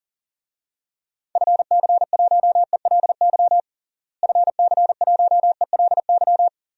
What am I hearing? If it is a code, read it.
Morse code, heard as a keyed tone, about 40 wpm, 700 Hz. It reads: FC1ELY FC1ELY
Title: DXCC-100 Call Signs - 6 Characters - Repeated 2x - 40wpm